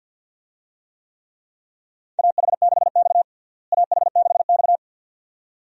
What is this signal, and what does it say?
Morse code, heard as a keyed tone, about 49 wpm, 700 Hz. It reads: AH6X AH6X
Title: USA Call Signs - 4 Characters - Repeated 2x - 50wpm